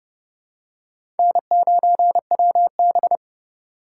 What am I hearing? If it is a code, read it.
Morse code, heard as a keyed tone, about 30 wpm, 700 Hz. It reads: N9WB
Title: USA Call Signs - 4 Characters - 8x Character Spacing - 30wpm